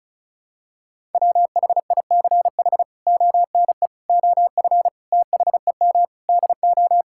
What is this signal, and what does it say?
Morse code, heard as a keyed tone, about 34 wpm, 700 Hz. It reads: WHICH ONE OF THEM DO
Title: Sentences from Top 1000 Words - Review Part 5 - 35wpm